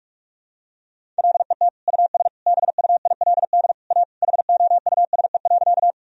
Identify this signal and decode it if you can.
Morse code, heard as a keyed tone, about 41 wpm, 700 Hz. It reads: LET US BUILD A HOUSE.
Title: Sentences from Top 1000 Words - Review Part 1 - 45wpm